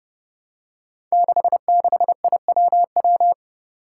Morse code, heard as a keyed tone, about 30 wpm, 700 Hz. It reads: 66IWW
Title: Five Challenging Letter-Number Combinations 30wpm